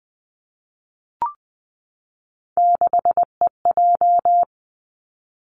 Morse code, heard as a keyed tone, about 20 wpm, 700 Hz. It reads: 6EJ
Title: Three Challenging Letter-Number Combinations 20wpm